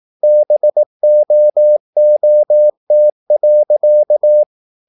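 Morse code, heard as a keyed tone, about 18 wpm, 600 Hz. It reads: BOOT.
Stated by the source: Morse practice at 18 wpm, English text, from CW med Gnister.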